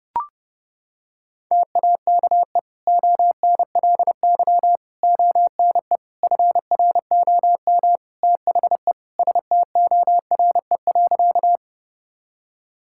Morse code, heard as a keyed tone, about 32 wpm, 700 Hz. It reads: TAKE ONLY ONE FROM THE STORE.
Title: Sentences from Top 1000 Words - Review Part 5 - 30wpm